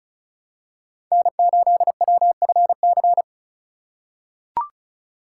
Morse code, heard as a keyed tone, about 36 wpm, 700 Hz. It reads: N8WFC
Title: USA Call Signs - 5 Characters - 4x Character Spacing - 35wpm